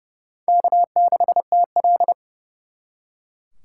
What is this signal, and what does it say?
Morse code, heard as a keyed tone, about 37 wpm, 700 Hz. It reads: K6TL
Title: Speed Racing - USA 1x2 Call Signs 30wpm